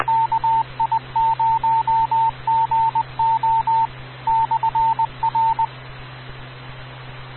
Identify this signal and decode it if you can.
Morse code, heard as a keyed tone, about 20 wpm, 900 Hz. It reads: KI0GO /R
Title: Rocky Mountain Radio League Repeaters